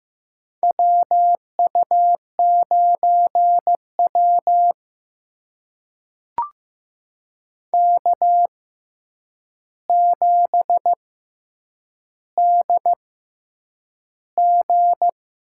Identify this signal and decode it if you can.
Morse code, heard as a keyed tone, about 15 wpm, spaced as for 8 wpm, 700 Hz. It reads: WU9W K7DG
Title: USA Call Signs - 6x Character Spacing - 15wpm